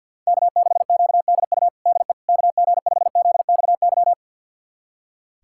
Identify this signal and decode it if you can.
Morse code, heard as a keyed tone, about 46 wpm, 700 Hz. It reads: K6QDU DE KG5ZXY
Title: USA Callsign DE Callsign 50wpm